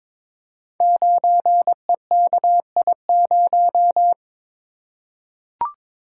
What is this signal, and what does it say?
Morse code, heard as a keyed tone, about 22 wpm, 700 Hz. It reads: 9EKI0
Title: Code Groups - 5 Letter:Numbers - 22wpm